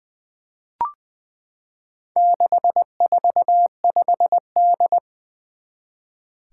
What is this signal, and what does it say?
Morse code, heard as a keyed tone, about 20 wpm, 700 Hz. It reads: 645D
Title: Four Challenging Letter-Number Combinations 20wpm